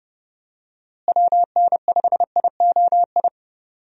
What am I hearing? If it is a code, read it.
Morse code, heard as a keyed tone, about 30 wpm, 700 Hz. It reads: WN5IOI
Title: USA Call Signs - 6 Characters - Repeated 2x - 30wpm